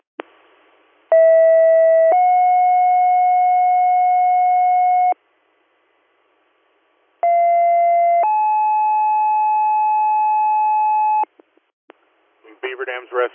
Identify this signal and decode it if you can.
Two-tone page, heard as a one-second tone, then a three-second tone, about 650 then 730 Hz, about 690 then 860 Hz, with speech. It is heard from 1.1 to 11.2 s.